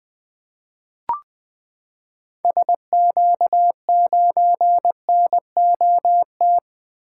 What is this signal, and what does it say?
Morse code, heard as a keyed tone, about 20 wpm, 700 Hz. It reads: SQ9NOT
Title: 2023 SOTA Activator Call Signs - 20wpm